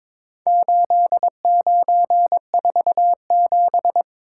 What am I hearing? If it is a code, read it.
Morse code, heard as a keyed tone, about 23 wpm, 700 Hz. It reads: 8947
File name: practice / 22wpm / Four Letter-Number Combinations - 22wpm